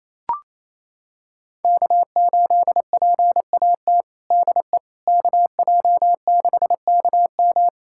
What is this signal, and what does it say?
Morse code, heard as a keyed tone, about 27 wpm, 700 Hz. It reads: K8PAT DE KJ6KM
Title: USA Callsign DE Callsign - 5 Characters - 28wpm